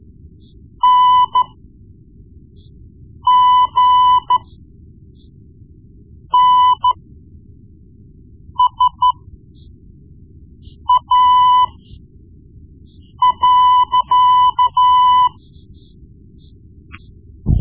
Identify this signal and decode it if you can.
Morse code, heard as a keyed tone, about 13 wpm, spaced as for 4 wpm, 1000 Hz. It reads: NGNSA.
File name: Morse Lessons / Morse Lessons / Tape-1B